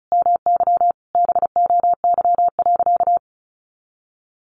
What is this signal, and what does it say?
Morse code, heard as a keyed tone, about 34 wpm, 700 Hz. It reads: MY BOY.
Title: Sentences from Top 1000 Words 35wpm